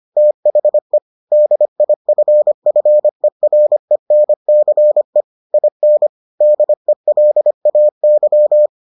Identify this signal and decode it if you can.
Morse code, heard as a keyed tone, about 25 wpm, 600 Hz. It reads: THE DIFFERENCE IN DELAY